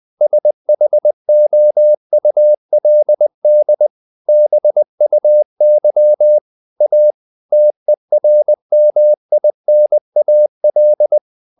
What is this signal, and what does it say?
Morse code, heard as a keyed tone, about 20 wpm, 600 Hz. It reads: SHOULD BUY A TERMINAL
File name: the old man and the sea / the old man and the sea part 1 20wpm